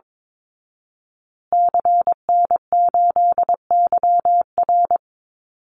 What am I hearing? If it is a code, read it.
Morse code, heard as a keyed tone, about 22 wpm, 700 Hz. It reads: CN8YR
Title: Call Signs Repeated 3x - 22wpm